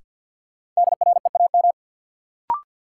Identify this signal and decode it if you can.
Morse code, heard as a keyed tone, about 52 wpm, 700 Hz. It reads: DREAM
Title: Sentences from Top 1500 Words - New Word Drill - 50wpm